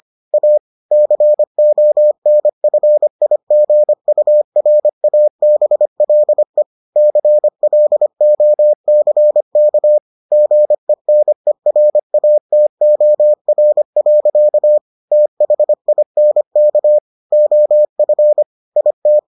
Morse code, heard as a keyed tone, about 25 wpm, 600 Hz. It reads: A CONFIGURABLE CLOCK GENERATOR. THINK OF IT